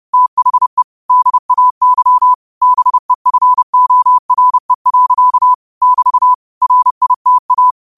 Morse code, heard as a keyed tone, about 30 wpm, 1000 Hz. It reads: THE DAY BEFORE. <BT> RITA